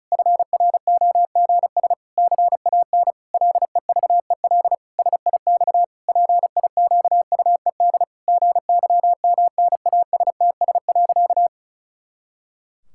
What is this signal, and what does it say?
Morse code, heard as a keyed tone, about 36 wpm, 700 Hz. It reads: FROGS CAN LEVEL SIX PIQUED GYMNASTS.
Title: Pangram Challenge v2 - 35wpm